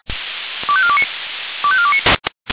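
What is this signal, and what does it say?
Five-tone selective call (ZVEI), heard as quick runs of five tones, tones of about 70 ms. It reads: 25429 25429
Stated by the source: A ZVEI five-tone selective call, code 25418, received with an SDR.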